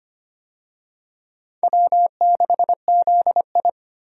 Morse code, heard as a keyed tone, about 25 wpm, 700 Hz. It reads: W6ZI
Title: USA Call Signs - 4 Characters - 25wpm